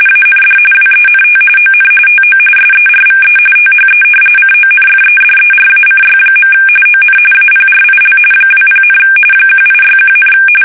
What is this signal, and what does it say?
RTTY, about 45.45 baud. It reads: AND WENT ON TO SERVE THE CONFEDERACY. ON APRIL 20, 1861,
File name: fsk850 / usnavy3